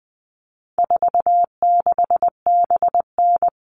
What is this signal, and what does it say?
Morse code, heard as a keyed tone, about 20 wpm, 700 Hz. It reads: 46BN
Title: All Cut Numbers - 4-Digit 20wpm